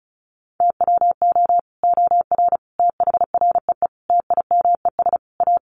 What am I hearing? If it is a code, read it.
Morse code, heard as a keyed tone, about 34 wpm, 700 Hz. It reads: TWO OR THREE TIMES A
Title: Sets of 5 Words Encore2 35wpm